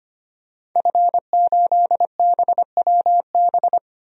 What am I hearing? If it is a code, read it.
Morse code, heard as a keyed tone, about 25 wpm, 700 Hz. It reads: F8BWB
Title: DXCC-100 Call Signs - 5 Characters - 25wpm